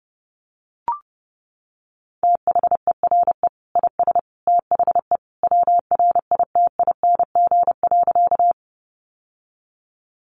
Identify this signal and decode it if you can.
Morse code, heard as a keyed tone, about 30 wpm, 700 Hz. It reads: THERE IS THE WRITING.
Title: Sentences from Top 800 Words 30wpm